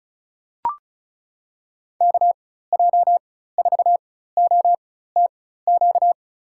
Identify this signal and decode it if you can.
Morse code, heard as a keyed tone, about 34 wpm, spaced as for 16 wpm, 700 Hz. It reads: KJ4OTQ
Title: USA Call Signs - 6 Characters - 4x Character Spacing - 35wpm